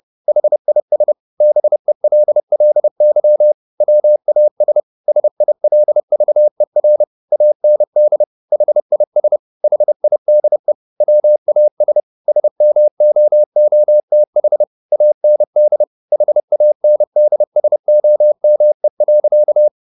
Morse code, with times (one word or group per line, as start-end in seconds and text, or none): 0.3-1.1 HIS
1.4-3.5 BELLY
3.8-4.8 WAS
5.1-7.0 SILVER
7.3-8.2 AND
8.5-9.4 HIS
9.6-10.7 HIDE
11.0-12.0 WAS
12.3-14.6 SMOOTH
14.9-15.8 AND
16.1-19.7 HANDSOME.